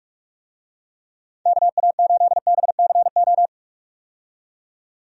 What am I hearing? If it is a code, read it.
Morse code, heard as a keyed tone, about 46 wpm, 700 Hz. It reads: KA8BCO
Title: USA Call Signs - 6 Characters - 45wpm